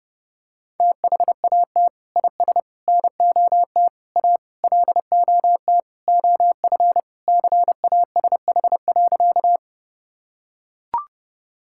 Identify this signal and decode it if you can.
Morse code, heard as a keyed tone, about 32 wpm, 700 Hz. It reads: THAT IS NOT A LOT OF CASH.
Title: Sentences from Top 1000 Words - Review Part 5 - 30wpm